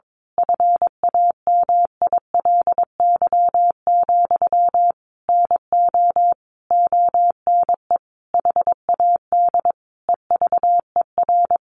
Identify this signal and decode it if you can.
Morse code, heard as a keyed tone, about 22 wpm, 700 Hz. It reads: FAMILY, NO ONE HAD EVER
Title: Sentences from Top 1200 Words 22wpm